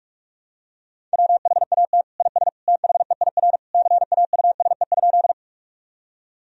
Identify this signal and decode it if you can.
Morse code, heard as a keyed tone, about 45 wpm, 700 Hz. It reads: WHAT IS THEIR CAUSE?
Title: Sentences from Top 1000 Words - Review Part 1 - 45wpm